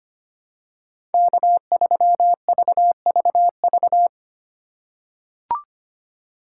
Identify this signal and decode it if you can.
Morse code, heard as a keyed tone, about 25 wpm, 700 Hz. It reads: K3VVV
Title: USA Call Signs - 2x Character Spacing - 25wpm